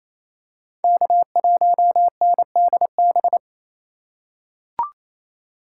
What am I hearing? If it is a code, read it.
Morse code, heard as a keyed tone, about 28 wpm, 700 Hz. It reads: K1NDB
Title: USA Call Signs - 5 Characters - 4x Character Spacing - 28wpm